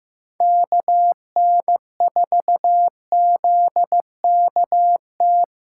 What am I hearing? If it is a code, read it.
Morse code, heard as a keyed tone, about 15 wpm, 700 Hz. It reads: KN4ZKT
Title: K1USN SST QSO Practice - 15wpm